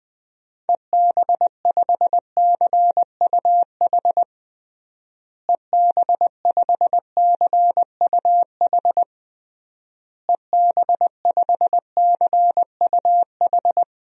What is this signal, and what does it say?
Morse code, heard as a keyed tone, about 21 wpm, 700 Hz. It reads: EB5CUH EB5CUH EB5CUH
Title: DXCC-100 Call Signs - 6 Characters - Repeated 3x - 20wpm